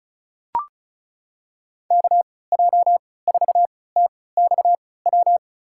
Morse code, heard as a keyed tone, about 34 wpm, spaced as for 20 wpm, 700 Hz. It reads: KJ4TXW
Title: USA Call Signs - 6 Characters - 3x Character Spacing - 35wpm